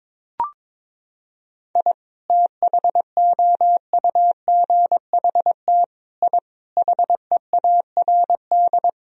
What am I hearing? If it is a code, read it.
Morse code, heard as a keyed tone, about 22 wpm, 700 Hz. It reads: I THOUGHT I HEARD
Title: Sentences from Top 1000 Words - Review Part 1 - 22wpm